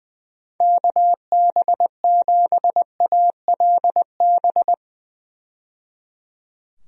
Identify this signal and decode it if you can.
Morse code, heard as a keyed tone, about 21 wpm, 700 Hz. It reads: KB7ALB
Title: USA Call Signs - 6 Characters - 20wpm